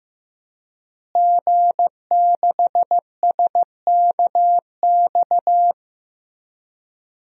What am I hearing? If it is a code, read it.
Morse code, heard as a keyed tone, about 15 wpm, 700 Hz. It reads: G6SKX
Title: DXCC-100 Call Signs - 5 Characters - Repeated 2x - 15wpm